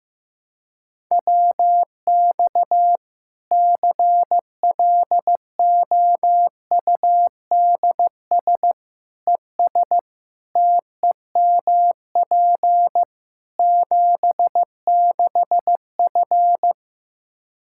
Morse code, heard as a keyed tone, about 15 wpm, 700 Hz. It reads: WX CLOUDS ES TEMP 76F
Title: K1USN SST QSO Practice 15wpm